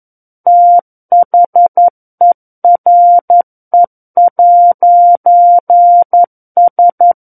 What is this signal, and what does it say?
Morse code, heard as a keyed tone, about 11 wpm, 700 Hz. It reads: THERE'S
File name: Book One / 02-Chapter 2 - 11 WPM